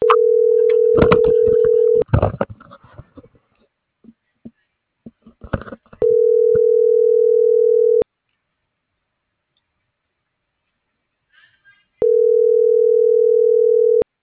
Ringback tone (North America), 440 + 480 Hz.